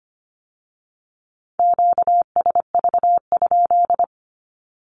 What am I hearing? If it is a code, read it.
Morse code, heard as a keyed tone, about 25 wpm, 700 Hz. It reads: QSV?
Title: Q-Codes with Spoken Meaning - 25wpm